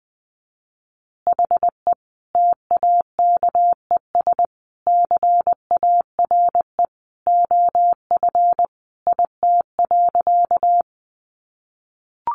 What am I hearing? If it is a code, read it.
Morse code, heard as a keyed tone, about 20 wpm, 700 Hz. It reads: HE TAKES CARE OF IT.
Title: Additional Easy Phrases 20wpm